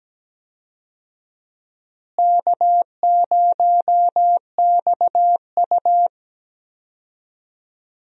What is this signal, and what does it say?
Morse code, heard as a keyed tone, about 17 wpm, 700 Hz. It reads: K0XU
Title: USA Call Signs - 4 Characters - Repeated 2x - 17wpm